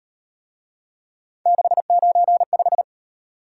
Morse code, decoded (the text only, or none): 695